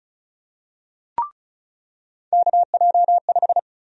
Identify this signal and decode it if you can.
Morse code, heard as a keyed tone, about 36 wpm, 700 Hz. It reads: KJ5
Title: USA Call Sign Prefixes - 35wpm